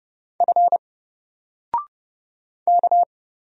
Morse code, heard as a keyed tone, about 30 wpm, 700 Hz. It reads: F K
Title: Single Letter-Number Alt 30wpm